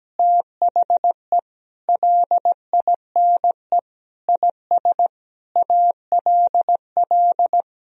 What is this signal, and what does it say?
Morse code, heard as a keyed tone, about 17 wpm, 700 Hz. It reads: THE LINE IS ALL
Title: Sentences from Top 1000 Words - Review Part 1 - 17wpm